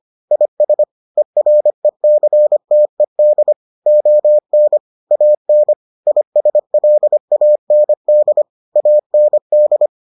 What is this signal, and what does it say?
Morse code, heard as a keyed tone, about 25 wpm, 600 Hz. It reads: IS ERECTED ON AN ISLAND AND